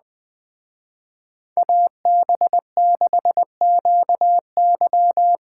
Morse code, heard as a keyed tone, about 19 wpm, 700 Hz. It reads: AB6QY
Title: Speed Racing - USA Call Signs 17wpm